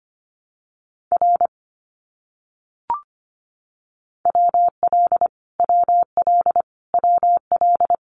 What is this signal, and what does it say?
Morse code, heard as a keyed tone, about 25 wpm, 700 Hz. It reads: R WL WL WL
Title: Abbreviations Repeated 3x - 25wpm